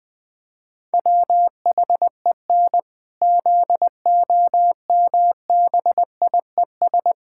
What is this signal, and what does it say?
Morse code, heard as a keyed tone, about 20 wpm, 700 Hz. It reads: WHEN ZOMBIES
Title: Pangram Challenge v2 - 20wpm